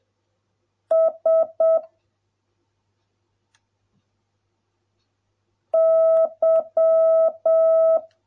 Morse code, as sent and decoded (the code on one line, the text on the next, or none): ... / -.--
S Y